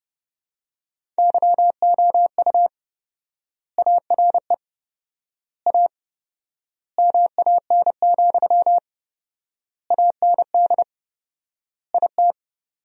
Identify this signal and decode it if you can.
Morse code, heard as a keyed tone, about 30 wpm, 700 Hz. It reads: YOU ARE A MAN, AND IT
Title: 4x Word-Spacing - Sentences from Top 200 Words - 30wpm